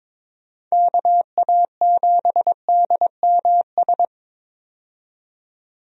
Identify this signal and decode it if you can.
Morse code, heard as a keyed tone, about 22 wpm, 700 Hz. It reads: KA7DMS
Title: USA Call Signs - 6 Characters - 22wpm